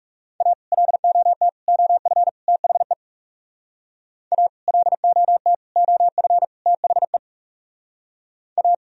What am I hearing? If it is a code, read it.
Morse code, heard as a keyed tone, about 40 wpm, 700 Hz. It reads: A LOT OF THE A LOT OF THE A